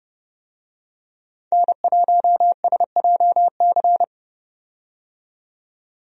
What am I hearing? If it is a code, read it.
Morse code, heard as a keyed tone, about 30 wpm, 700 Hz. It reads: N1SJC
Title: USA Call Signs - 5 Characters - 30wpm